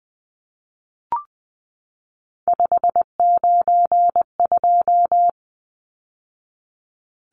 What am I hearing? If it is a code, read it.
Morse code, heard as a keyed tone, about 20 wpm, 700 Hz. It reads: 592
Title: Numbers - 3-Digit 20wpm